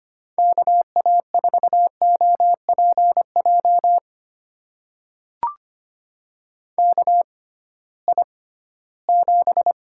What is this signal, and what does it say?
Morse code, heard as a keyed tone, about 25 wpm, spaced as for 16 wpm, 700 Hz. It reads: KA4OPJ KI7
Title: USA Call Signs - 6 Characters - 6x Character Spacing - 25wpm